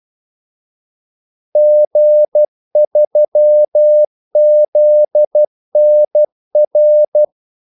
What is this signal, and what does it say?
Morse code, heard as a keyed tone, about 12 wpm, 600 Hz. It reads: G3ZNR
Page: 2 ekstra ordmellemrum mellem kaldesignalerne